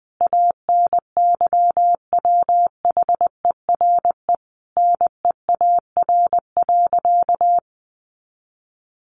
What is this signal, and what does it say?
Morse code, heard as a keyed tone, about 20 wpm, 700 Hz. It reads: ANYWHERE NEAR.
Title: Sentences from Top 1500 Words 20wpm